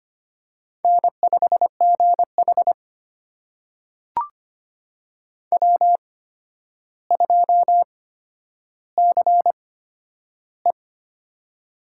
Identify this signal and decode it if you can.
Morse code, heard as a keyed tone, about 25 wpm, spaced as for 11 wpm, 700 Hz. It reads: N5GH W2CE
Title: USA Call Signs - 4 Characters - 8x Character Spacing - 25wpm